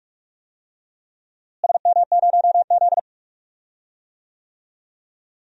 Morse code, heard as a keyed tone, about 46 wpm, 700 Hz. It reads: SM0Z
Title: DXCC-100 Call Signs - 4 Characters - Repeated 3x - 45wpm